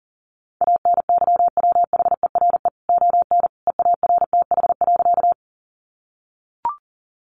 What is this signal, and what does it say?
Morse code, heard as a keyed tone, about 40 wpm, 700 Hz. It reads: ANYWHERE ON EARTH.
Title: Sentences from Top 1400 Words 40wpm